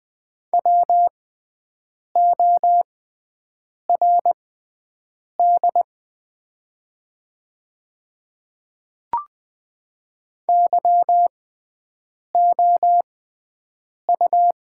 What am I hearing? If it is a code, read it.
Morse code, heard as a keyed tone, about 20 wpm, spaced as for 7 wpm, 700 Hz. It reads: WORD YOU
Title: 6x Character Spacing - Top 100 Words - 20wpm